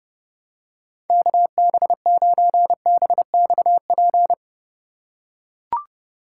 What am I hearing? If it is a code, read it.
Morse code, heard as a keyed tone, about 32 wpm, 700 Hz. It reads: KB9BXP
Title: USA Call Signs - 6 Characters - 3x Character Spacing - 30wpm